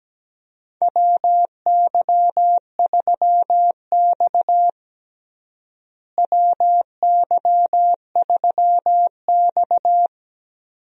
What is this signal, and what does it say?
Morse code, heard as a keyed tone, about 17 wpm, 700 Hz. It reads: WY3X WY3X
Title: USA Call Signs - 4 Characters - Repeated 2x - 17wpm